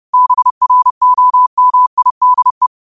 Morse code, heard as a keyed tone, about 30 wpm, 1000 Hz. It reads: BROMIDE
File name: qotdfiles / 30wpm